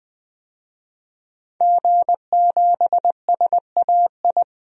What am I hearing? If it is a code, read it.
Morse code, heard as a keyed tone, about 20 wpm, 700 Hz. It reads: G7SAI
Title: DXCC-100 Call Signs - 5 Characters - 20wpm